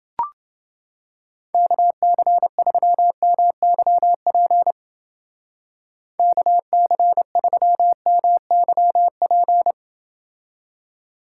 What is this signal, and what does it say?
Morse code, heard as a keyed tone, about 28 wpm, 700 Hz. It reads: KC3MYP KC3MYP